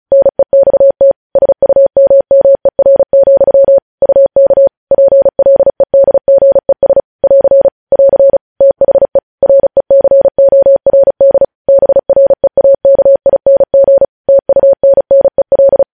Morse code, heard as a keyed tone, about 36 wpm, 550 Hz. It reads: NEXT SUMMER, UK PLEDGES <AR> <AR> THE RECORD BREAKING TUNNEL
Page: News Headlines in Morse Code at 35 WPM